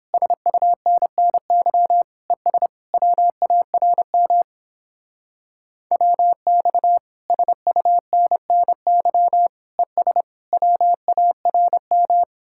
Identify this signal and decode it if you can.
Morse code, heard as a keyed tone, about 31 wpm, 700 Hz. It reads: SUNNY ES WARM WX SUNNY ES WARM